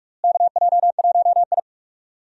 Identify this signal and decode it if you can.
Morse code, heard as a keyed tone, about 46 wpm, 700 Hz. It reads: KJ1I
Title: USA Call Signs - 4 Characters - Repeated 3x - 45wpm